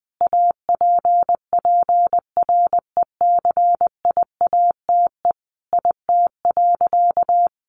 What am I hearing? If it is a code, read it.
Morse code, heard as a keyed tone, about 20 wpm, 700 Hz. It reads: APPRECIATE IT.
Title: Sentences from Top 1300 Words 20wpm